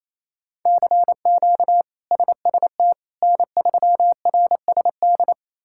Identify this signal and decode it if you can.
Morse code, heard as a keyed tone, about 28 wpm, 700 Hz. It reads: CQ SST N3RSD
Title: K1USN SST QSO Practice - 28wpm